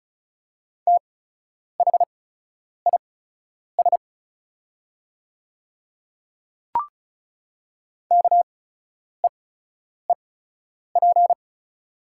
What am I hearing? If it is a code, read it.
Morse code, heard as a keyed tone, about 34 wpm, spaced as for 10 wpm, 700 Hz. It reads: THIS KEEP
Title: 8x Character Spacing - Top 300 Words - 35wpm